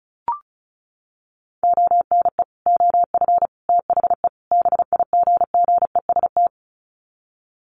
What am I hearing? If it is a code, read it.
Morse code, heard as a keyed tone, about 34 wpm, 700 Hz. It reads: ONE OF THE BIGGEST
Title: Sets of 4 Words 35wpm